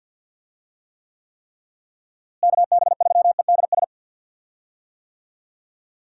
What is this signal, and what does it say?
Morse code, heard as a keyed tone, about 52 wpm, 700 Hz. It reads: KB3EDS
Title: USA Call Signs - 6 Characters - Repeated 3x - 50wpm